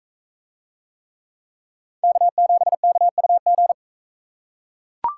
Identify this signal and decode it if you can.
Morse code, heard as a keyed tone, about 41 wpm, 700 Hz. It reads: K7KUG